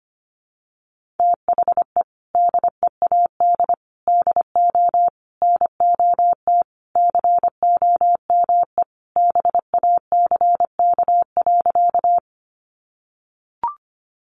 Morse code, heard as a keyed tone, about 25 wpm, 700 Hz. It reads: THE DEAD DO NOT COME BACK.
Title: Sentences from Top 1500 Words 25wpm